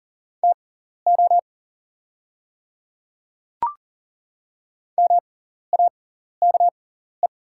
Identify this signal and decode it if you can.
Morse code, heard as a keyed tone, about 40 wpm, spaced as for 14 wpm, 700 Hz. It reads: TO MAKE
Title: 6x Character Spacing - Top 100 Words - 40wpm